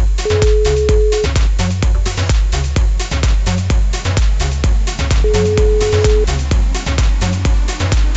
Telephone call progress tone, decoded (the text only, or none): ringback tone